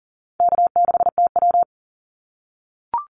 Morse code, heard as a keyed tone, about 40 wpm, 700 Hz. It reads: K6TW
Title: NA SOTA Activator Callsigns 40wpm